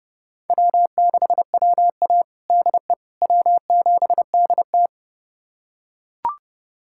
Morse code, heard as a keyed tone, about 30 wpm, 700 Hz. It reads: W6WA DE W7DT
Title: USA Callsign DE Callsign - 4 Characters - 30wpm